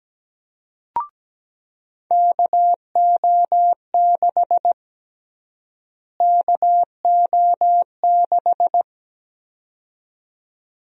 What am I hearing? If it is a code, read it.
Morse code, heard as a keyed tone, about 17 wpm, 700 Hz. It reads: KO6 KO6
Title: USA Call Sign Prefixes - Repeated 2x - 17wpm